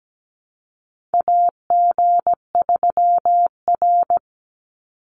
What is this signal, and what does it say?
Morse code, heard as a keyed tone, about 17 wpm, 700 Hz. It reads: AG3R
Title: Call Signs Repeated 3x - 17wpm